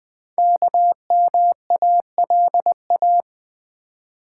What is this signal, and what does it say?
Morse code, heard as a keyed tone, about 20 wpm, 700 Hz. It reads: KMALA
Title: Code Groups - 5 Letters - 20wpm